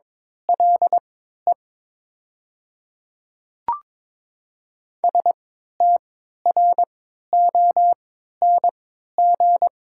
Morse code, heard as a keyed tone, about 22 wpm, spaced as for 13 wpm, 700 Hz. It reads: LE STRONG
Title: 3x Character Spacing - Top 500 Words - 22wpm